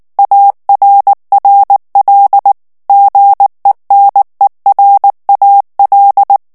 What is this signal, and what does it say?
Morse code, heard as a keyed tone, about 19 wpm, 800 Hz. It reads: ARRL GENERAL
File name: ANGLAIS texte clair 3 vitesse 18 mots minute